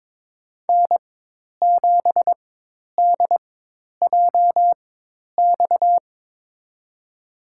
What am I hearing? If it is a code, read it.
Morse code, heard as a keyed tone, about 22 wpm, spaced as for 10 wpm, 700 Hz. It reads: N7DJX
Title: USA Call Signs - 4x Character Spacing - 22wpm